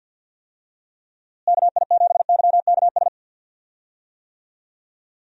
Morse code, heard as a keyed tone, about 52 wpm, 700 Hz. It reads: KI7YKS
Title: USA Call Signs - 6 Characters - Repeated 3x - 50wpm